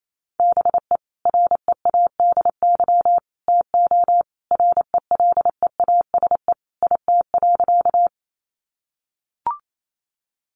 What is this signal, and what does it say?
Morse code, heard as a keyed tone, about 28 wpm, 700 Hz. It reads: BE READY TO RELEASE IT.
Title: Sentences from Top 1100 Words 28wpm